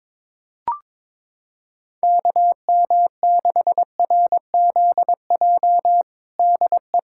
Morse code, heard as a keyed tone, about 22 wpm, 700 Hz. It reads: KM6RZJ DE
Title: USA Callsign DE Callsign - 6 Characters - 22wpm